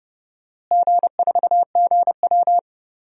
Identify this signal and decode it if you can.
Morse code, heard as a keyed tone, about 30 wpm, 700 Hz. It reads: G4GW
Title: DXCC-100 Call Signs - 4 Characters - Repeated 3x - 30wpm